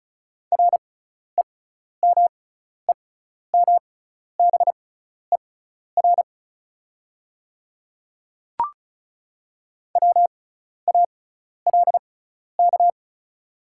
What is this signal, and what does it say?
Morse code, heard as a keyed tone, about 36 wpm, spaced as for 12 wpm, 690 Hz. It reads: REMEMBER WALK
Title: 6x Character Spacing - Top 500 Words - 35wpm